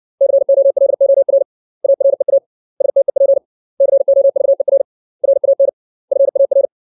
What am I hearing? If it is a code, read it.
Morse code, heard as a keyed tone, about 60 wpm, 550 Hz. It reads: COLOR AREA STEP COVER RAN FAR
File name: 632 words / 632 Words 60WPM